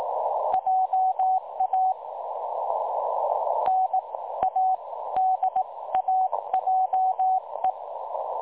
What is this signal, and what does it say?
Morse code, heard as a keyed tone, about 18 wpm, spaced as for 15 wpm, 750 Hz. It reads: JA NADAJE
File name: dlazaawansowanych / nagraniazpasma / websdr recording 3565.0kHz